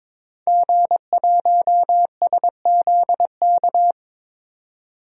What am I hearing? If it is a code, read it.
Morse code, heard as a keyed tone, about 22 wpm, 700 Hz. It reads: G1SZK
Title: DXCC-100 Call Signs - 5 Characters - 22wpm